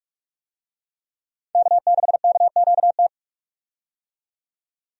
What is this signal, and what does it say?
Morse code, heard as a keyed tone, about 46 wpm, 700 Hz. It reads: K6KQT
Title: USA Call Signs - 5 Characters - 45wpm